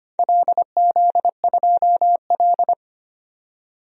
Morse code, heard as a keyed tone, about 25 wpm, 700 Hz. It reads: LZ2L
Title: DXCC-100 Call Signs - 4 Characters - 25wpm